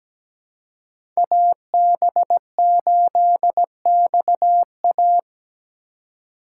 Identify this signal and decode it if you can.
Morse code, heard as a keyed tone, about 17 wpm, 700 Hz. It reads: AB8XA
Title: USA Call Signs - 5 Characters - 2x Character Spacing - 17wpm